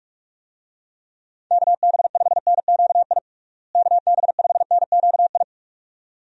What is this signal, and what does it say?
Morse code, heard as a keyed tone, about 42 wpm, 700 Hz. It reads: KB5NQI KB5NQI
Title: USA Call Signs - 6 Characters - Repeated 2x - 45wpm